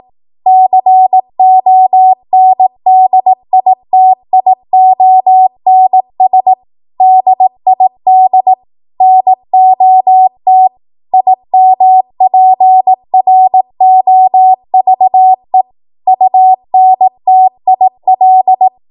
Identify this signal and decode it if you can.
Morse code, heard as a keyed tone, about 18 wpm, 750 Hz. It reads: CONDITIONS DID NOT IMPROVE UNTIL
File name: Archive / 18 WPM / 18WPM